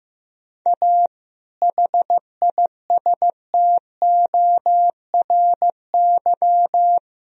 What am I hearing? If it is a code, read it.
Morse code, heard as a keyed tone, about 15 wpm, 700 Hz. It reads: A HISTORY
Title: Sentences from Top 1000 Words - Review Part 3 - 15wpm